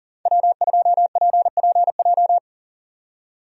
Morse code, heard as a keyed tone, about 38 wpm, 700 Hz. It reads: W2PPJ
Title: USA Call Signs - 5 Characters - 40wpm